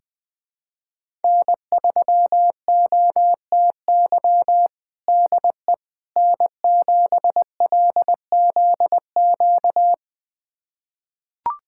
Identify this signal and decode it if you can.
Morse code, heard as a keyed tone, about 21 wpm, 700 Hz. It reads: N3OTY DE N7LZQ
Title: USA Callsign DE Callsign - 5 Characters - 20wpm